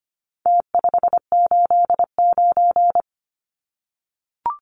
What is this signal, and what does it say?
Morse code, heard as a keyed tone, about 25 wpm, 700 Hz. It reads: T589
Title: Numbers - Contest Style - 4-Digit 25wpm